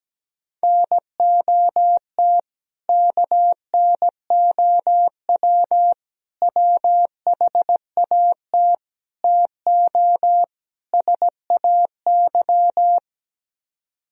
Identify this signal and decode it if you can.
Morse code, heard as a keyed tone, about 17 wpm, 700 Hz. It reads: NOT KNOW WHAT TO SAY
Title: Speed-Racing - Sets of 5 Words 17wpm